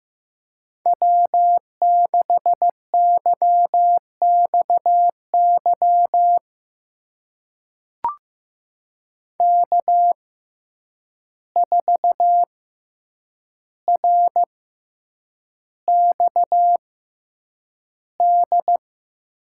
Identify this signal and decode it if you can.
Morse code, heard as a keyed tone, about 15 wpm, spaced as for 8 wpm, 700 Hz. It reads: W6YXY K4RXD
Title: USA Call Signs - 5 Characters - 6x Character Spacing - 15wpm